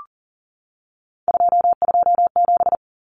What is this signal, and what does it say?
Morse code, heard as a keyed tone, about 40 wpm, 700 Hz. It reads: 227
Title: Common Cut Numbers - 3-Digit 40wpm